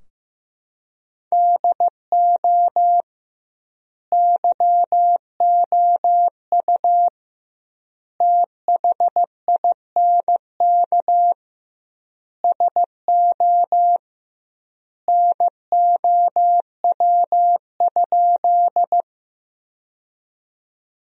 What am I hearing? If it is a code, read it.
Morse code, heard as a keyed tone, about 15 wpm, 700 Hz. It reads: DO YOU THINK SO NOW?
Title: 2x Word-Spacing - Sentences from Top 200 Words - 15wpm